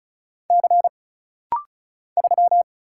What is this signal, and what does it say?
Morse code, heard as a keyed tone, about 36 wpm, 700 Hz. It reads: C 3
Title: Single Letter-Number - Rapid-Fire 35wpm